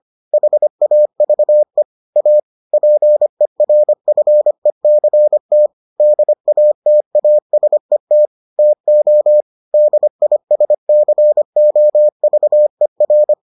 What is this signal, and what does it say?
Morse code, heard as a keyed tone, about 25 wpm, 600 Hz. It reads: HAVE A PERFECT DATASET TO DISCOVER